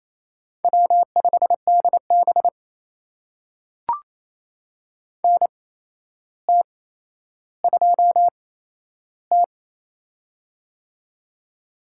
Morse code, heard as a keyed tone, about 28 wpm, spaced as for 12 wpm, 700 Hz. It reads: W5DB NT2T
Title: USA Call Signs - 4 Characters - 8x Character Spacing - 28wpm